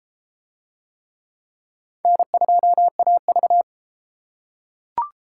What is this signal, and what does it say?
Morse code, heard as a keyed tone, about 31 wpm, 700 Hz. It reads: N2AV